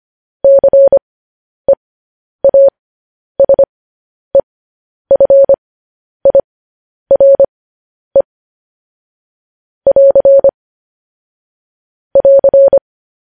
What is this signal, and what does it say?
Morse code, heard as a keyed tone, about 25 wpm, spaced as for 10 wpm, 550 Hz. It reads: CEASEFIRE <AR> <AR>
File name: podcast / News headlines-1x-25x10